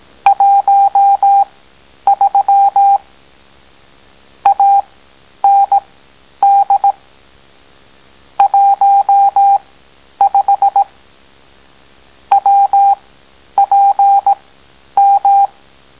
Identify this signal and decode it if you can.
Morse code, heard as a keyed tone, about 18 wpm, spaced as for 10 wpm, 800 Hz. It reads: 13 AND 15 WPM